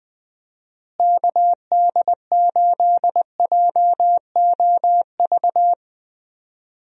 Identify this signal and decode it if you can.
Morse code, heard as a keyed tone, about 20 wpm, 700 Hz. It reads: KD8JOV
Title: USA Call Signs - 6 Characters - Repeated 3x - 20wpm